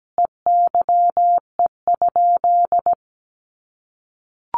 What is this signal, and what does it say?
Morse code, heard as a keyed tone, about 17 wpm, 700 Hz. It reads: EYE?
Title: Sentences from Top 500 Words 17wpm